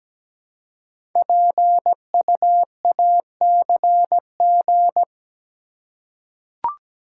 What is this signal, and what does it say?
Morse code, heard as a keyed tone, about 17 wpm, 700 Hz. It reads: PUACG
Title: Code Groups - 5 Letters - 17wpm